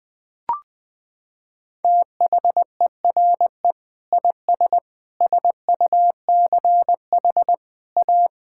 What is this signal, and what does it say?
Morse code, heard as a keyed tone, about 21 wpm, 700 Hz. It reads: THERE IS SUCH A
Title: Sets of 6 Words - 20wpm